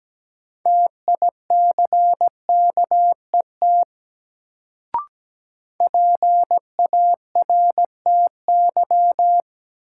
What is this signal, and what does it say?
Morse code, heard as a keyed tone, about 17 wpm, 700 Hz. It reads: TICKET PARTY
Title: Sentences from Top 1400 Words - New Word Drill Alt - 17wpm